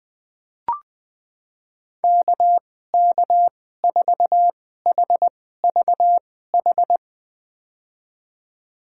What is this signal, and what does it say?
Morse code, heard as a keyed tone, about 20 wpm, spaced as for 14 wpm, 700 Hz. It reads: KK4HVH
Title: USA Call Signs - 6 Characters - 2x Character Spacing - 20wpm